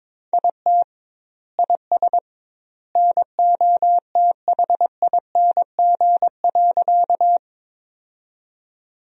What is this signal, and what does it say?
Morse code, heard as a keyed tone, about 22 wpm, 700 Hz. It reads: IT IS NOTHING.
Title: 2x Word-Spacing - Sentences from Top 500 Words - 22wpm